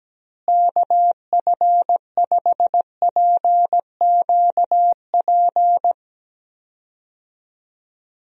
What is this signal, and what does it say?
Morse code, heard as a keyed tone, about 17 wpm, 700 Hz. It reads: KF5PQP
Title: USA Call Signs - 6 Characters - Repeated 2x - 17wpm